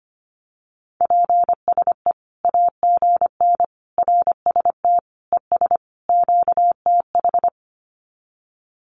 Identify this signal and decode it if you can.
Morse code, heard as a keyed tone, about 25 wpm, 700 Hz. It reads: PSE AGN RST ES QTH
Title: Common QSO Phrases 25wpm